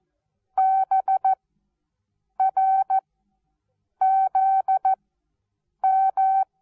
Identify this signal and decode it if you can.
Morse code, heard as a keyed tone, about 14 wpm, 770 Hz. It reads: B R Z M
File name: Onlinelehrgang / morsen / DJ4UF Morsen19